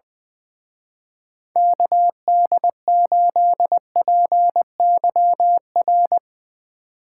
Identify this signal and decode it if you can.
Morse code, heard as a keyed tone, about 21 wpm, 700 Hz. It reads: KD8PYR
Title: Speed Racing - USA 2x3 Call Signs 20wpm